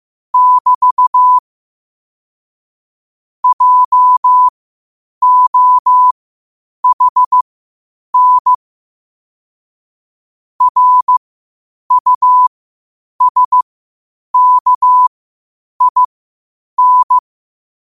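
Morse code, heard as a keyed tone, about 15 wpm, spaced as for 8 wpm, 1000 Hz. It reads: <BT> JOHN RUSKIN
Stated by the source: Morse practice at 8 WPM from Morse Resource.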